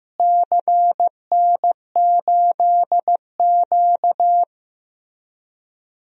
Morse code, heard as a keyed tone, about 16 wpm, 700 Hz. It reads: CN8Q
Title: DXCC-100 Call Signs - 4 Characters - Repeated 2x - 15wpm